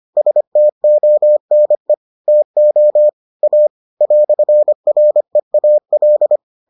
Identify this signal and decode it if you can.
Morse code, heard as a keyed tone, about 25 wpm, 600 Hz. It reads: STONE TO A "REAL